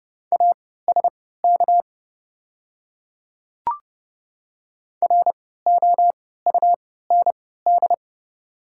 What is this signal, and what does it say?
Morse code, heard as a keyed tone, about 30 wpm, spaced as for 17 wpm, 690 Hz. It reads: ASK ROUND
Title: 3x Character Spacing - Top 200 Words - 30wpm